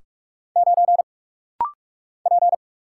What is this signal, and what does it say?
Morse code, heard as a keyed tone, about 44 wpm, 700 Hz. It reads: 9 P
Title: Single Letter-Number - Rapid-Fire 45wpm